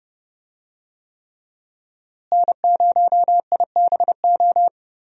Morse code, heard as a keyed tone, about 30 wpm, 700 Hz. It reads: N0IBO
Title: USA Call Signs - 5 Characters - Repeated 3x - 30wpm